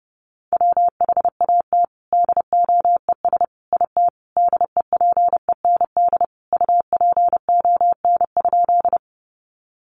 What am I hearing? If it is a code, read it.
Morse code, heard as a keyed tone, about 30 wpm, 700 Hz. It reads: WHAT DOES IT DEPEND UPON?
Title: Sentences from Top 1300 Words 30wpm